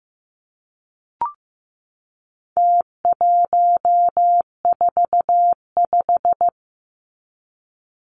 Morse code, heard as a keyed tone, about 15 wpm, 700 Hz. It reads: T145
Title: Numbers - Contest Style - 4-Digit 15wpm